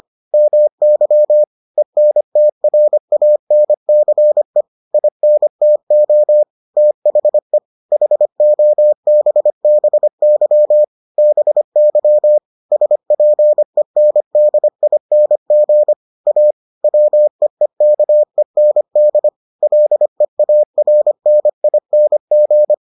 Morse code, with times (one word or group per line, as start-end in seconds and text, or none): 0.3-1.4 MY
1.8-4.6 ENTRANCE
4.9-6.4 INTO
6.8-7.6 THE
7.9-10.8 HOBBY
11.2-12.4 BY
12.7-15.9 SPENDING
16.3-16.5 A
16.8-19.3 WEEKEND
19.6-22.7 LEARNING